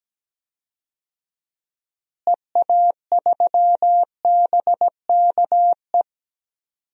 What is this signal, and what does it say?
Morse code, heard as a keyed tone, about 17 wpm, 700 Hz. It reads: EA3BKE
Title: DXCC-100 Call Signs - 6 Characters - Repeated 3x - 17wpm